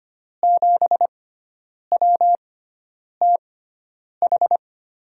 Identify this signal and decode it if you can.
Morse code, heard as a keyed tone, about 25 wpm, spaced as for 9 wpm, 700 Hz. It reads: 7WTH
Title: USA Call Signs - 6 Characters - 6x Character Spacing - 25wpm